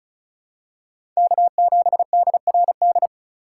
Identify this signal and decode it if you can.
Morse code, heard as a keyed tone, about 36 wpm, 700 Hz. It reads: K7DRD
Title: USA Call Signs - 3x Character Spacing - 35wpm